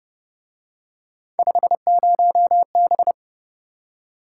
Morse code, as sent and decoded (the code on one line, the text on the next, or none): ..... ----- -...
50B